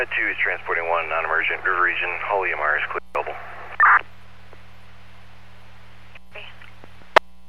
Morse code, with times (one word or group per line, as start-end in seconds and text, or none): none